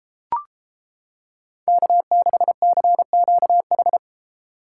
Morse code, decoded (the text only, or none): K6CQH